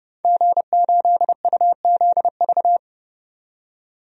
Morse code, heard as a keyed tone, about 29 wpm, 700 Hz. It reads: G8UZV
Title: DXCC-100 Call Signs - 5 Characters - 30wpm